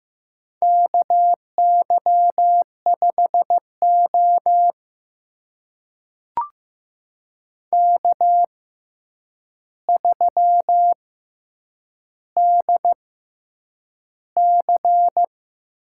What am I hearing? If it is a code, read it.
Morse code, heard as a keyed tone, about 15 wpm, spaced as for 8 wpm, 700 Hz. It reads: KY5O K3DC
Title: USA Call Signs - 4 Characters - 6x Character Spacing - 15wpm